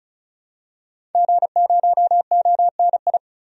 Morse code, heard as a keyed tone, about 36 wpm, 700 Hz. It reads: G0ONI
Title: DXCC-100 Call Signs - 5 Characters - 35wpm